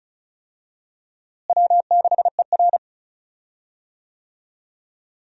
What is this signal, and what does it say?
Morse code, heard as a keyed tone, about 36 wpm, 700 Hz. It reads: W6ER
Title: USA Call Signs - 4 Characters - Repeated 2x - 35wpm